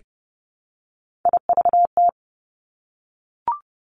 Morse code, heard as a keyed tone, about 30 wpm, 700 Hz. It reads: IVT
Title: Three Challenging Letter-Number Combinations 30wpm